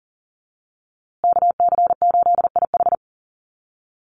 Morse code, heard as a keyed tone, about 40 wpm, 700 Hz. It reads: KC8IH
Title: USA Call Signs Repeated 2x 40wpm